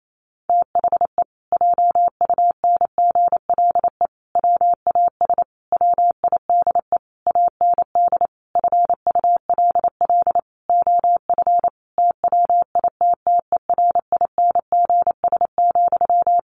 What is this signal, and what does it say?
Morse code, heard as a keyed tone, about 28 wpm, 700 Hz. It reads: THE JUNGLE WAS WIDE AND FULL OF TWITTERINGS,
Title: A Sound Of Thunder 28wpm